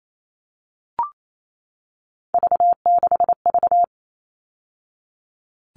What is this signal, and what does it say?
Morse code, heard as a keyed tone, about 28 wpm, 700 Hz. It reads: V6V
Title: All Cut Numbers - 3-Digit 28wpm